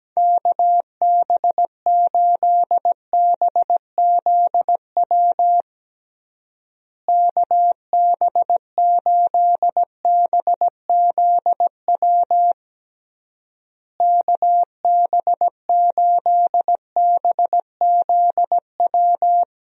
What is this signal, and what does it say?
Morse code, heard as a keyed tone, about 17 wpm, 700 Hz. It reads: KB8BZW KB8BZW KB8BZW
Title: USA Call Signs - 6 Characters - Repeated 3x - 17wpm